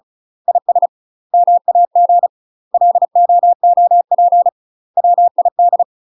Morse code, heard as a keyed tone, about 36 wpm, 700 Hz. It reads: IS MAG LOOP WID
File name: UK MASTER QSOs TRAINING / 35WPM / UK MASTER QSOs 0007